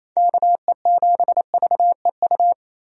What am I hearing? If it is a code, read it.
Morse code, heard as a keyed tone, about 28 wpm, 700 Hz. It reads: KE7VEU
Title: USA Call Signs - 6 Characters - Repeated 2x - 28wpm